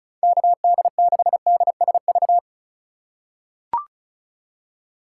Morse code, heard as a keyed tone, about 36 wpm, 700 Hz. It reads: KD6DSV
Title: USA Call Signs - 6 Characters - 3x Character Spacing - 35wpm